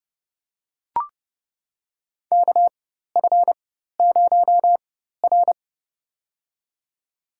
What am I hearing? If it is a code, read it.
Morse code, heard as a keyed tone, about 30 wpm, spaced as for 14 wpm, 700 Hz. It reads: KF0R
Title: USA Call Signs - 4 Characters - 4x Character Spacing - 30wpm